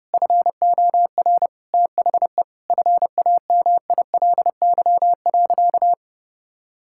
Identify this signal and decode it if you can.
Morse code, heard as a keyed tone, about 32 wpm, 700 Hz. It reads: FOR THE FAMILY.
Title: Sentences from Top 1000 Words - Review Part 5 - 30wpm